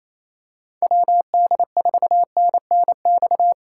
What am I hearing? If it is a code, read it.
Morse code, heard as a keyed tone, about 27 wpm, 700 Hz. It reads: WD4NNX
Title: USA Call Signs - 6 Characters - 4x Character Spacing - 28wpm